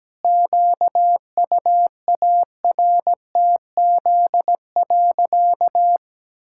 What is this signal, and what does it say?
Morse code, heard as a keyed tone, about 17 wpm, 700 Hz. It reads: QUARTZ.
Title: Pangram Challenge v2 - 17wpm